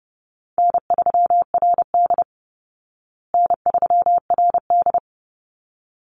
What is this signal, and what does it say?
Morse code, heard as a keyed tone, about 30 wpm, 700 Hz. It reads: N3RD N3RD
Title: Call Signs Repeated 3x - 30wpm